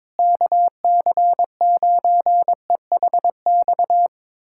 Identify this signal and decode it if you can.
Morse code, heard as a keyed tone, about 22 wpm, 700 Hz. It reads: KC9EHX
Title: USA Callsign DE Callsign - 6 Characters - 22wpm